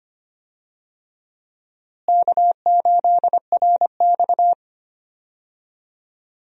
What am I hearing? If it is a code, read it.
Morse code, heard as a keyed tone, about 25 wpm, 700 Hz. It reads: K8RX
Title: USA Call Signs - 4 Characters - Repeated 2x - 25wpm